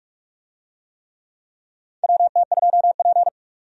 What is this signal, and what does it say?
Morse code, heard as a keyed tone, about 46 wpm, 700 Hz. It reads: WT2P
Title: USA Call Signs - 4 Characters - 45wpm